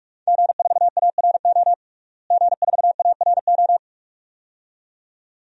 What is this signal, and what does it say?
Morse code, heard as a keyed tone, about 46 wpm, 700 Hz. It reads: G4ARO G4ARO
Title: DXCC-100 Call Signs - 5 Characters - Repeated 2x - 45wpm